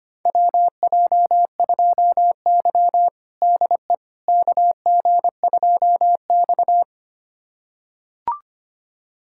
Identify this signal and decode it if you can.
Morse code, heard as a keyed tone, about 25 wpm, 700 Hz. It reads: WJ2Y DE KG2X
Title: USA Callsign DE Callsign - 4 Characters - 25wpm